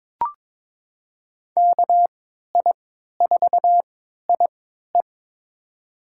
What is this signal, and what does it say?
Morse code, heard as a keyed tone, about 22 wpm, spaced as for 12 wpm, 700 Hz. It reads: KI4IE
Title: USA Call Signs - 5 Characters - 3x Character Spacing - 22wpm